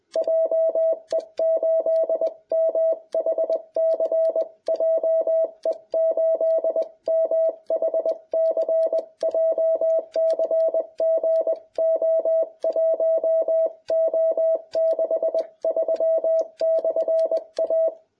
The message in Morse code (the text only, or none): JE8MHC JE8MHC JCG O1O63/A